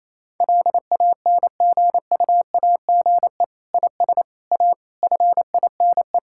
Morse code, heard as a keyed tone, about 28 wpm, 690 Hz. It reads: LANGUAGE IS A FINE
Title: Sentences from Top 1000 Words - Review Part 6 - 28wpm